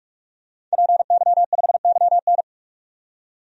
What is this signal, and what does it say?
Morse code, heard as a keyed tone, about 46 wpm, 700 Hz. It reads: PY5YN
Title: DXCC-100 Call Signs - 5 Characters - 45wpm